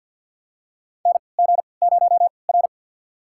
Morse code, heard as a keyed tone, about 50 wpm, spaced as for 28 wpm, 700 Hz. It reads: NG0R
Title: USA Call Signs - 4 Characters - 3x Character Spacing - 50wpm